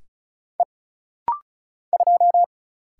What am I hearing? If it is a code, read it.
Morse code, heard as a keyed tone, about 33 wpm, 700 Hz. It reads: E 2
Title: Single Letter-Number Alt - Rapid-Fire 35wpm